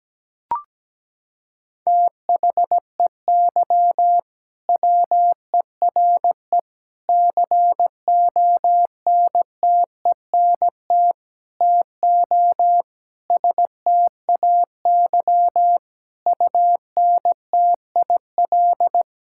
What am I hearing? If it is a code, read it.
Morse code, heard as a keyed tone, about 17 wpm, 700 Hz. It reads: THEY WERE CONTENT TO STAY UNTIL
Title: Sentences from Top 1000 Words - Review Part 6 - 17wpm